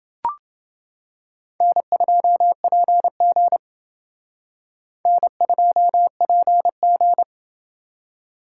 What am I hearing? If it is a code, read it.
Morse code, heard as a keyed tone, about 28 wpm, 700 Hz. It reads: N2PG N2PG